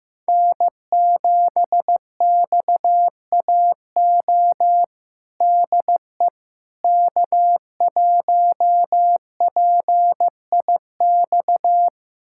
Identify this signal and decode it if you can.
Morse code, heard as a keyed tone, about 15 wpm, 700 Hz. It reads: N7XAO DE K1PIX
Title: USA Callsign DE Callsign - 5 Characters - 15wpm